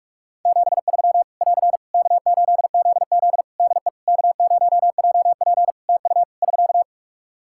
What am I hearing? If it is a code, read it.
Morse code, heard as a keyed tone, about 41 wpm, 700 Hz. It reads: 73 <AR> K8ZZ DE K0JP TU <SK>
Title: Practice QSO - Exchange 4 - Ending - 45wpm